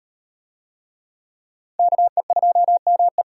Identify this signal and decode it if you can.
Morse code, heard as a keyed tone, about 38 wpm, 700 Hz. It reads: KE2ME